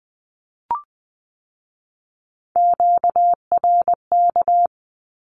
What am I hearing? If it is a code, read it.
Morse code, heard as a keyed tone, about 20 wpm, 700 Hz. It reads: QRK
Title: Q-Codes with Spoken Meaning - 20wpm